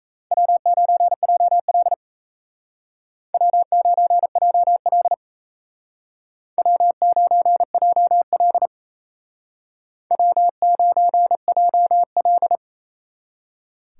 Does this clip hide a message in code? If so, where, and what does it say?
Yes: Morse code from 0.3 to 12.6 s, heard as a keyed tone, about 35 wpm, 700 Hz. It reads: W9JL W9JL W9JL W9JL